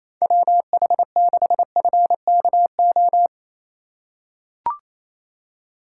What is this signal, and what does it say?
Morse code, heard as a keyed tone, about 28 wpm, 700 Hz. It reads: WH6FKO
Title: USA Call Signs - 6 Characters - 3x Character Spacing - 28wpm